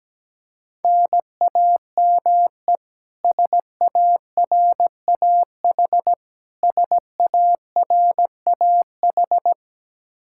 Morse code, heard as a keyed tone, about 17 wpm, 700 Hz. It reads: NAME SARAH SARAH
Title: Practice QSO - Exchange 1 - Beginning - 17wpm